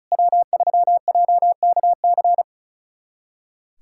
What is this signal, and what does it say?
Morse code, heard as a keyed tone, about 36 wpm, 700 Hz. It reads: W3JKC
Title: USA Callsign DE Callsign 35wpm